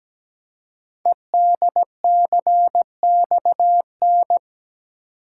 Morse code, heard as a keyed tone, about 17 wpm, 700 Hz. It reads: EDCXN
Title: Code Groups - 5 Letters - 17wpm